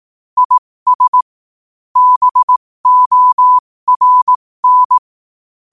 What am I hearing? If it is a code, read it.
Morse code, heard as a keyed tone, about 18 wpm, spaced as for 15 wpm, 1000 Hz. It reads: IS BORN
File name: qotdfiles / 15wpm